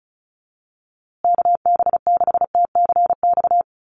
Morse code, heard as a keyed tone, about 34 wpm, 700 Hz. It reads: KB6TCX
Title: USA Call Signs Repeated 2x 35wpm